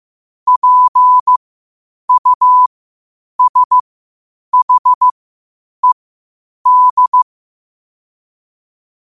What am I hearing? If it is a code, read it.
Morse code, heard as a keyed tone, about 15 wpm, spaced as for 8 wpm, 1000 Hz. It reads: PUSHED
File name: qotdfiles / 8wpm